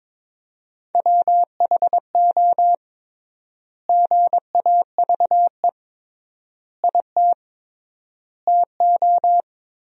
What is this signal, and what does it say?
Morse code, heard as a keyed tone, about 22 wpm, 700 Hz. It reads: WHO GAVE IT TO
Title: 3x Word-Spacing - Sentences from Top 500 Words - 22wpm